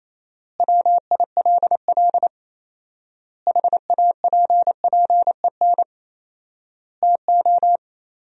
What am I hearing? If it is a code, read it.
Morse code, heard as a keyed tone, about 28 wpm, 700 Hz. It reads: WILL HAPPEN TO
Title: 4x Word-Spacing - Sentences from Top 500 Words - 28wpm